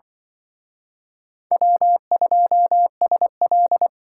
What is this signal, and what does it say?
Morse code, heard as a keyed tone, about 24 wpm, 700 Hz. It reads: W2SL
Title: Speed Racing - USA Call Signs 20wpm